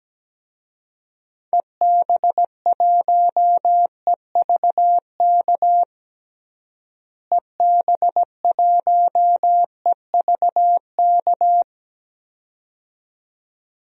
Morse code, heard as a keyed tone, about 17 wpm, 700 Hz. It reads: EB1EVK EB1EVK
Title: DXCC-100 Call Signs - 6 Characters - Repeated 2x - 17wpm